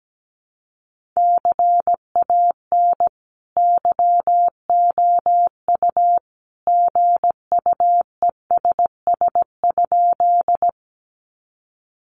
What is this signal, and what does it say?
Morse code, heard as a keyed tone, about 17 wpm, 700 Hz. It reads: CAN YOU GUESS?
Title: Sentences from Top 1500 Words 17wpm